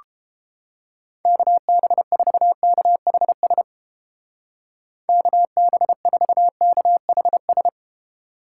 Morse code, heard as a keyed tone, about 31 wpm, 700 Hz. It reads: KB4KHS KB4KHS